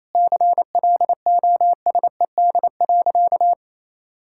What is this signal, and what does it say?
Morse code, heard as a keyed tone, about 28 wpm, 700 Hz. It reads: CLOSED.
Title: Sentences from Top 1600 Words 28wpm